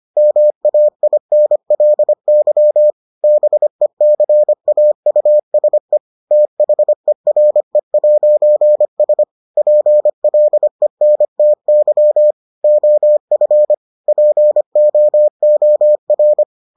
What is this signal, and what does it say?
Morse code, heard as a keyed tone, about 25 wpm, 600 Hz. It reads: MAINLY BECAUSE THERE'S PLENTY OF POOR